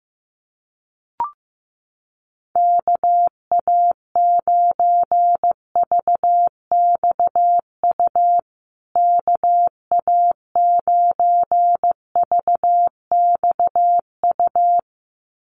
Morse code, heard as a keyed tone, about 15 wpm, 700 Hz. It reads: KA9VXU KA9VXU
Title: USA Call Signs Repeated 2x 15wpm